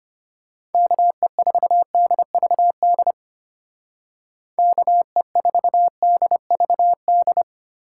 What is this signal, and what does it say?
Morse code, heard as a keyed tone, about 29 wpm, 700 Hz. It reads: KE4DVD KE4DVD
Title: Speed Racing - USA 2x3 Call Signs 25wpm